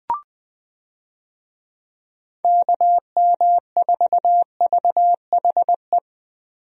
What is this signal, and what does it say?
Morse code, heard as a keyed tone, about 21 wpm, 700 Hz. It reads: KM4VHE
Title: USA Call Signs - 6 Characters - Repeated 2x - 20wpm